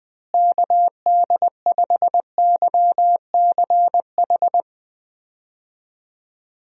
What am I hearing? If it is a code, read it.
Morse code, heard as a keyed tone, about 21 wpm, 700 Hz. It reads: KD5YCH
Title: USA Call Signs - 6 Characters - Repeated 3x - 20wpm